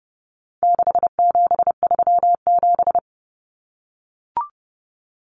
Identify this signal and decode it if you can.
Morse code, heard as a keyed tone, about 30 wpm, 700 Hz. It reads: 6737
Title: Common Cut Numbers - 4-Digit 30wpm